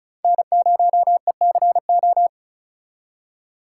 Morse code, heard as a keyed tone, about 46 wpm, 700 Hz. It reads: N0ECO